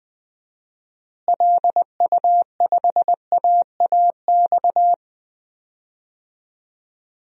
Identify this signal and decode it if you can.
Morse code, heard as a keyed tone, about 20 wpm, 700 Hz. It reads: LU5AAX
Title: DXCC-100 Call Signs - 6 Characters - Repeated 3x - 20wpm